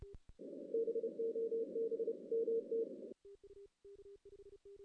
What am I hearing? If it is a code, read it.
Morse code, heard as a keyed tone, about 30 wpm, 460 Hz. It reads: BOB MT
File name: 2016 JAN CW NAQP / 2016 Jan naqp k5ot - 0-8